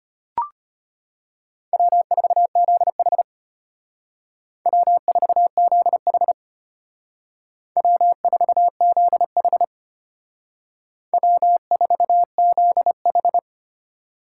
Morse code, heard as a keyed tone, about 33 wpm, 700 Hz. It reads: W4ZH W4ZH W4ZH W4ZH